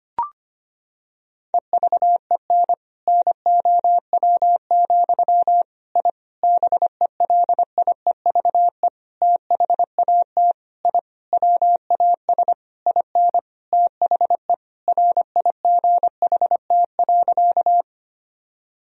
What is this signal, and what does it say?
Morse code, heard as a keyed tone, about 25 wpm, 700 Hz. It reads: EVEN NOW, I BELIEVE THAT I WAS IN THE RIGHT.
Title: Sentences from Top 1000 Words - Review Part 1 - 25wpm